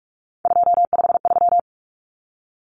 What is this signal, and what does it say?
Morse code, heard as a keyed tone, about 44 wpm, 700 Hz. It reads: 253
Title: Numbers - 3-Digit 45wpm